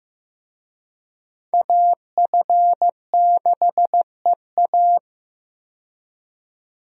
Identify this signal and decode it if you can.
Morse code, heard as a keyed tone, about 15 wpm, 700 Hz. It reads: AF6EA
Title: USA Call Signs - 5 Characters - 15wpm